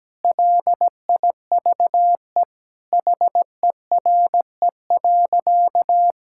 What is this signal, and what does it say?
Morse code, heard as a keyed tone, about 17 wpm, 700 Hz. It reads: LIVE HERE.
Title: Sentences from Top 1000 Words - Review Part 6 - 17wpm